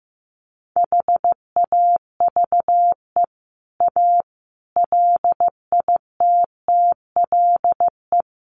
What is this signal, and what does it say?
Morse code, heard as a keyed tone, about 15 wpm, 700 Hz. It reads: HAVE A LITTLE
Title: Sentences from Top 1300 Words 15wpm